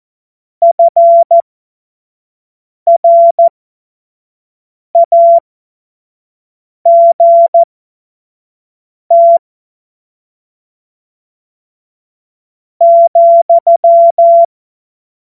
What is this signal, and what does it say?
Morse code, heard as a keyed tone, about 14 wpm, spaced as for 4 wpm, 680 Hz. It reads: FRAGT,
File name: Morsekurs / Witze / Witz 17. Mai 2022 25 F70